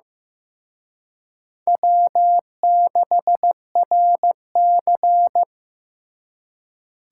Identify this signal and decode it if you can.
Morse code, heard as a keyed tone, about 18 wpm, 700 Hz. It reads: W6RC
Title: Speed Racing - USA 1x2 Call Signs 15wpm